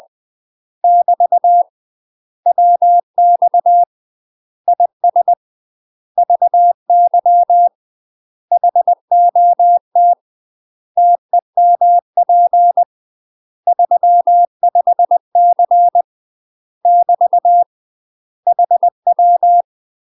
Morse code, heard as a keyed tone, about 20 wpm, 700 Hz. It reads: <BT> WX IS VY HOT TEMP 35C <BT> HW
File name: UK MASTER QSOs TRAINING/20WPM